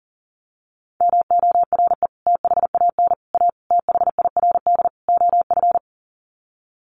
Morse code, heard as a keyed tone, about 40 wpm, 700 Hz. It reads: MORE THAN A THIRD OF
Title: Sets of 5 Words Encore2 40wpm